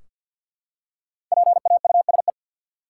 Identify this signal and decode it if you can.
Morse code, heard as a keyed tone, about 46 wpm, 700 Hz. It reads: PAUSE
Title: Five Letter Words - 50wpm